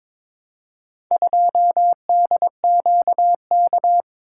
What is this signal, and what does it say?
Morse code, heard as a keyed tone, about 23 wpm, 700 Hz. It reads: 2DQK
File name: practice / 22wpm / Four Letter-Number Combinations - 22wpm